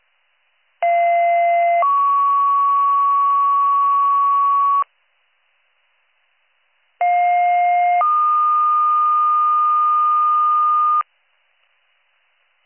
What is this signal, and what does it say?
Two-tone page, heard as a one-second tone, then a three-second tone, about 690 then 1120 Hz, about 710 then 1220 Hz. It is heard from 0.8 to 11.0 s.